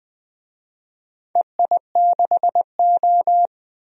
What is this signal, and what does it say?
Morse code, heard as a keyed tone, about 20 wpm, 700 Hz. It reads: EI6O
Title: DXCC-100 Call Signs - 4 Characters - 20wpm